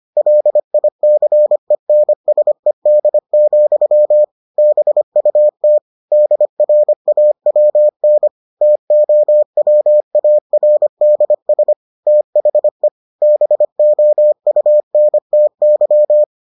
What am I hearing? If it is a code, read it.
Morse code, heard as a keyed tone, about 25 wpm, 600 Hz. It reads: LICENSED, BUT DRAWN TOWARDS THE BOUNTY